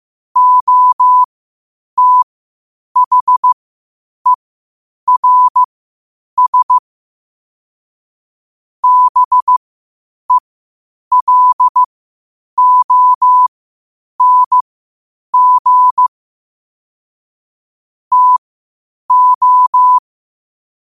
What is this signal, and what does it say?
Morse code, heard as a keyed tone, about 15 wpm, spaced as for 8 wpm, 1000 Hz. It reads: OTHERS BELONG TO
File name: 8wpm